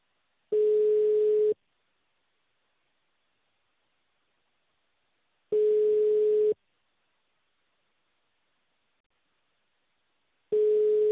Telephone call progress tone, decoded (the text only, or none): ringback tone